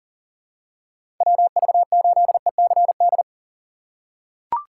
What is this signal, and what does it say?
Morse code, heard as a keyed tone, about 39 wpm, 700 Hz. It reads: WV8ECD
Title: USA Call Signs - 6 Characters - 4x Character Spacing - 40wpm